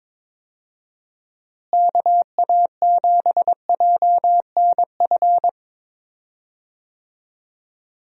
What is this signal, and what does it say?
Morse code, heard as a keyed tone, about 22 wpm, 700 Hz. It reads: KA7JNF
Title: USA Call Signs - 6 Characters - Repeated 2x - 22wpm